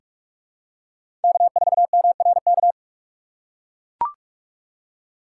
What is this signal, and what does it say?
Morse code, heard as a keyed tone, about 44 wpm, 700 Hz. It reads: K4MRK
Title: USA Call Signs - 5 Characters - 6x Character Spacing - 45wpm